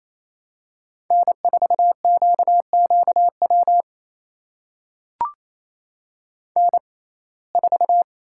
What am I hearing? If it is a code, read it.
Morse code, heard as a keyed tone, about 28 wpm, spaced as for 20 wpm, 700 Hz. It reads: N4QQW N4
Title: USA Call Signs - 6x Character Spacing - 28wpm